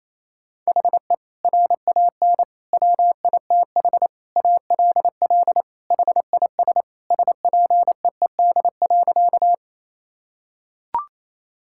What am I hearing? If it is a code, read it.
Morse code, heard as a keyed tone, about 28 wpm, 700 Hz. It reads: HE RAN WITH ALL HIS SPEED.
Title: Sentences from Top 1000 Words - Review Part 5 - 28wpm